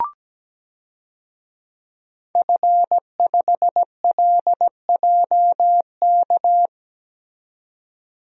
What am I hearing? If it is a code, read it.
Morse code, heard as a keyed tone, about 17 wpm, 700 Hz. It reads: F5LJK
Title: DXCC-100 Call Signs - 5 Characters - 17wpm